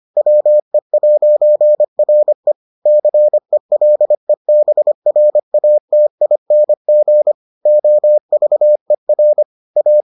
Morse code, heard as a keyed tone, about 25 wpm, 600 Hz. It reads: WE'RE CELEBRATING OVER A